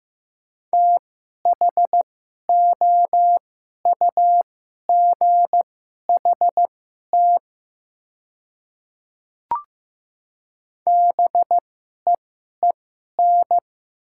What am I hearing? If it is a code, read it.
Morse code, heard as a keyed tone, about 15 wpm, spaced as for 11 wpm, 700 Hz. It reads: THOUGHT BEEN
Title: 2x Character Spacing - Top 300 Words - 15wpm